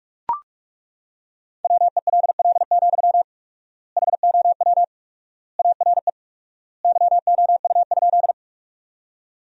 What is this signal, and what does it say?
Morse code, heard as a keyed tone, about 41 wpm, 700 Hz. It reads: WELL, HOW ARE YOU?
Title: 4x Word-Spacing - Sentences from Top 200 Words - 45wpm